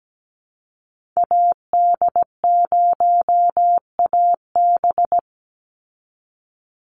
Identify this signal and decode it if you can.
Morse code, heard as a keyed tone, about 17 wpm, 700 Hz. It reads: AD0AB
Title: Call Signs Repeated 3x - 17wpm